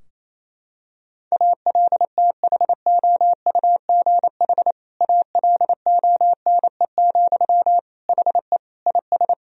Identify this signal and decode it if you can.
Morse code, heard as a keyed tone, about 27 wpm, 700 Hz. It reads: ALTHOUGH ALONE, HE IS
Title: Sentences from Top 1000 Words - Review Part 3 - 28wpm